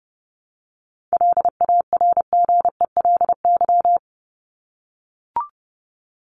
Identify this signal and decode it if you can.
Morse code, heard as a keyed tone, about 30 wpm, 700 Hz. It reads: LARGELY
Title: LY Suffix - Top 100 Words with Base Words 30wpm